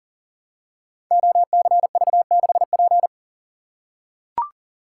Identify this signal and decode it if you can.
Morse code, heard as a keyed tone, about 41 wpm, 700 Hz. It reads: OCV6P
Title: Code Groups - 5 Letter:Numbers - 40wpm